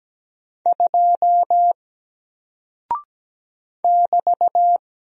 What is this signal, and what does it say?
Morse code, heard as a keyed tone, about 17 wpm, 700 Hz. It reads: 2 <BT>
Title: Single Character (Letter:Number:Punctuation:Prosign) Alt 17wpm